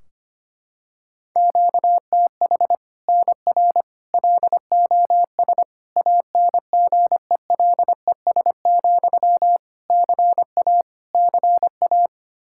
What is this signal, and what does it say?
Morse code, heard as a keyed tone, about 25 wpm, 700 Hz. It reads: QTH NR LOS ANGELES, CA CA
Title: Practice QSO - Exchange 1 - Beginning - 25wpm